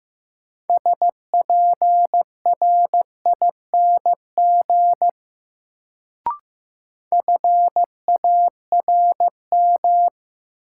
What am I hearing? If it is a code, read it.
Morse code, heard as a keyed tone, about 15 wpm, 700 Hz. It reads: SPRING FARM
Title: Sentences from Top 1200 Words - New Word Drill Alt - 15wpm